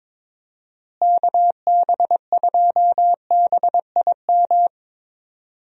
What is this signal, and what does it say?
Morse code, heard as a keyed tone, about 22 wpm, 700 Hz. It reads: KB2BIM
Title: USA Call Signs - 6 Characters - Repeated 2x - 22wpm